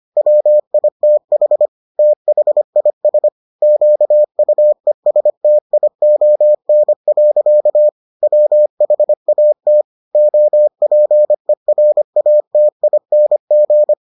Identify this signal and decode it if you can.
Morse code, heard as a keyed tone, about 25 wpm, 600 Hz. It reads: WITH THIS QUESTION. WHAT OPERATING